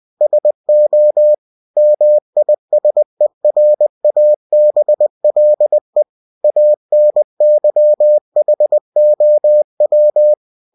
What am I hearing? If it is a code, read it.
Morse code, heard as a keyed tone, about 20 wpm, 600 Hz. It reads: SO MISERABLE ANYHOW
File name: cw-books / winnie-the-pooh / winnie-20wpm-page-0092